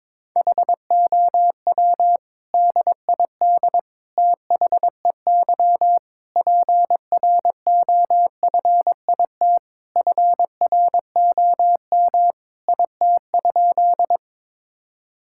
Morse code, heard as a keyed tone, about 22 wpm, 700 Hz. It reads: HOW DID THEY PROFIT FROM IT?
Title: Sentences from Top 1000 Words - Review Part 5 - 22wpm